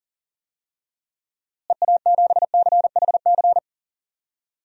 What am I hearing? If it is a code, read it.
Morse code, heard as a keyed tone, about 38 wpm, 700 Hz. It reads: EA7CHC
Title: DXCC-100 Call Signs - 6 Characters - Repeated 3x - 40wpm